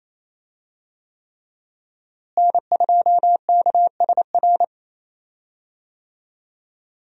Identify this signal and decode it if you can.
Morse code, heard as a keyed tone, about 27 wpm, 700 Hz. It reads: N2KSR
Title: USA Call Signs - 5 Characters - Repeated 3x - 28wpm